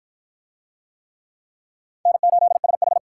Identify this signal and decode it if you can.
Morse code, heard as a keyed tone, about 51 wpm, 700 Hz. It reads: N8SH